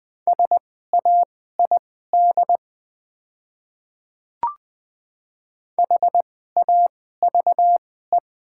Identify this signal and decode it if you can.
Morse code, heard as a keyed tone, about 20 wpm, spaced as for 14 wpm, 690 Hz. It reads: SAID HAVE
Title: 2x Character Spacing - Top 300 Words - 20wpm